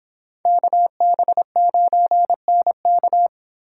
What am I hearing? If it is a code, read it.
Morse code, heard as a keyed tone, about 26 wpm, 700 Hz. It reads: KB9NK